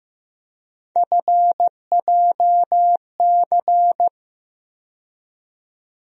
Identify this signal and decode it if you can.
Morse code, heard as a keyed tone, about 15 wpm, 700 Hz. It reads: FJC
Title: USA Call Sign Postfixes - 3 Letters 15wpm